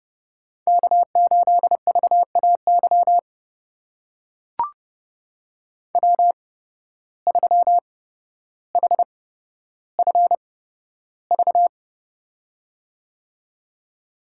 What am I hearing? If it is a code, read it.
Morse code, heard as a keyed tone, about 30 wpm, spaced as for 13 wpm, 700 Hz. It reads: K8VAY W3HFV
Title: USA Call Signs - 5 Characters - 8x Character Spacing - 30wpm